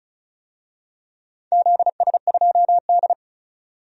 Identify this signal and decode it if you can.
Morse code, heard as a keyed tone, about 36 wpm, 700 Hz. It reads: ZS2D
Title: DXCC-100 Call Signs - 4 Characters - 35wpm